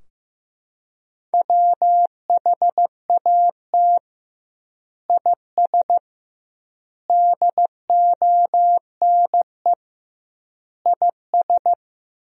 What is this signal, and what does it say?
Morse code, heard as a keyed tone, about 15 wpm, 700 Hz. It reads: WHAT IS DONE IS
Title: 2x Word-Spacing - Sentences from Top 500 Words - 15wpm